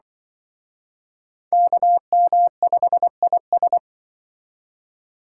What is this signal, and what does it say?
Morse code, heard as a keyed tone, about 24 wpm, 700 Hz. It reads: KM5IS
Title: Speed Racing - USA 2x2 Call Signs 20wpm